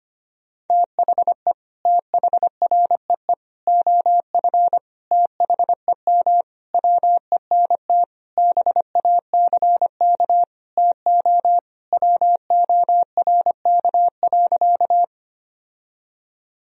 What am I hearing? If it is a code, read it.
Morse code, heard as a keyed tone, about 25 wpm, 700 Hz. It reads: THE THREE OF THEM WENT BACK TO WORK.
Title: Sentences from Top 1000 Words - Review Part 1 - 25wpm